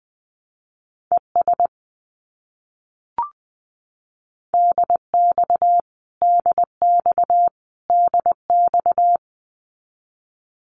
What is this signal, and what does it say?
Morse code, heard as a keyed tone, about 20 wpm, 700 Hz. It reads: ES DX DX DX
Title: Abbreviations Repeated 3x - 20wpm